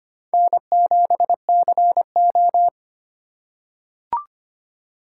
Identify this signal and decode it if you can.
Morse code, heard as a keyed tone, about 25 wpm, 690 Hz. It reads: N7CO
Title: USA Call Signs - 4 Characters - 4x Character Spacing - 25wpm